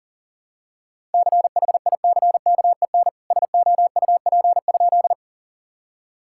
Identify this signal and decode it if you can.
Morse code, heard as a keyed tone, about 40 wpm, 700 Hz. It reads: CHICKEN SOUP?
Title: Sentences from Top 1000 Words - Review Part 6 - 40wpm